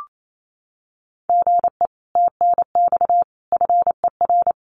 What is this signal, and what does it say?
Morse code, heard as a keyed tone, about 28 wpm, 700 Hz. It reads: GE TNX FER
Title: Common QSO Phrases Encore 28wpm